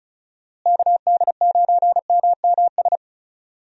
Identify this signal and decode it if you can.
Morse code, heard as a keyed tone, about 36 wpm, 700 Hz. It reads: KD9MMS
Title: USA Call Signs - 6 Characters - Repeated 3x - 35wpm